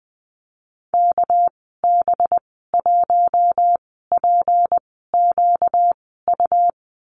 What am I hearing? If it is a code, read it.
Morse code, heard as a keyed tone, about 20 wpm, spaced as for 14 wpm, 700 Hz. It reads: KB1PQU
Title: 238 Lesson 38g - Letter q - Call Signs 20wpm